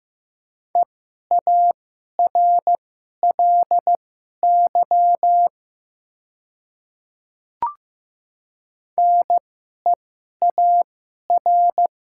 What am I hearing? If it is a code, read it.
Morse code, heard as a keyed tone, about 15 wpm, spaced as for 11 wpm, 700 Hz. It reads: EARLY NEAR
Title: 2x Character Spacing - Top 500 Words - 15wpm